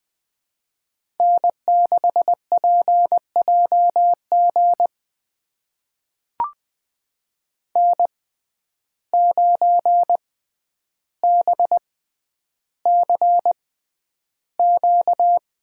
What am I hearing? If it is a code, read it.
Morse code, heard as a keyed tone, about 20 wpm, spaced as for 10 wpm, 700 Hz. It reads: N6PJG N9BCQ
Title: USA Call Signs - 5 Characters - 6x Character Spacing - 20wpm